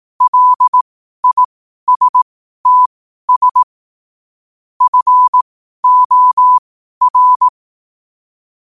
Morse code, heard as a keyed tone, about 18 wpm, spaced as for 12 wpm, 1000 Hz. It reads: LISTS FOR
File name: qotdfiles / 12wpm